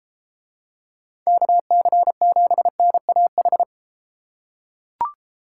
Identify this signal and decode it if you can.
Morse code, heard as a keyed tone, about 32 wpm, 700 Hz. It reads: KC7NAH